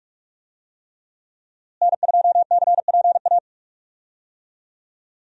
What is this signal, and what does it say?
Morse code, heard as a keyed tone, about 41 wpm, 700 Hz. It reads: N2CPA
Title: USA Call Signs - 5 Characters - Repeated 3x - 45wpm